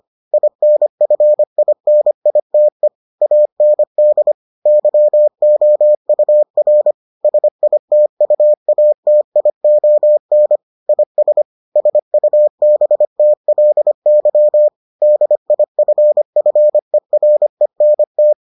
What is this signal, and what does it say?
Morse code, heard as a keyed tone, about 25 wpm, 600 Hz. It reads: INFINITE AND YOUR SITUATION IS SUBTLY DIFFERENT